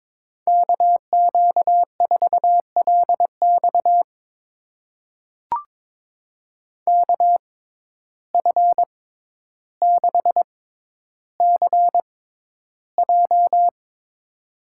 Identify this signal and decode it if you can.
Morse code, heard as a keyed tone, about 22 wpm, spaced as for 11 wpm, 700 Hz. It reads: KQ4LX KF6CJ
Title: USA Call Signs - 5 Characters - 6x Character Spacing - 22wpm